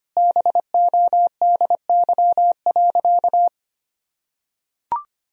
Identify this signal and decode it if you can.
Morse code, heard as a keyed tone, about 25 wpm, 700 Hz. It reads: BODY.
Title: Sentences from Top 1000 Words - Review Part 4 - 25wpm